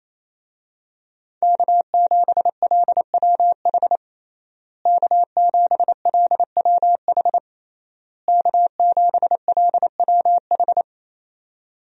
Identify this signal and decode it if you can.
Morse code, heard as a keyed tone, about 28 wpm, 700 Hz. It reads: K7LWH K7LWH K7LWH
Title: USA Call Signs - 5 Characters - Repeated 3x - 28wpm